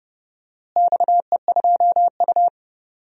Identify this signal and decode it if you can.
Morse code, heard as a keyed tone, about 30 wpm, 700 Hz. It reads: XE2U
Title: DXCC-100 Call Signs - 4 Characters - Repeated 3x - 30wpm